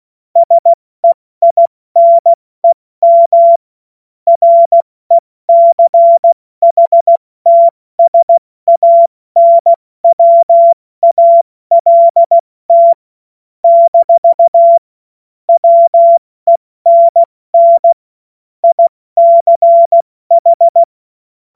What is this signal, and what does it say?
Morse code, heard as a keyed tone, about 16 wpm, 680 Hz. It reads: SEINEM RECHTSANWALT - WENN ICH
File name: Morsekurs / Witze / Witz 29. August 2023 70 F80